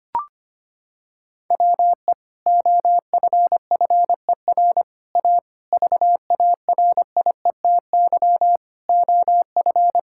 Morse code, heard as a keyed tone, about 25 wpm, 700 Hz. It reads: WE OFFER A VARIETY OF
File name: Sentences with 3-Word Phrases - 25wpm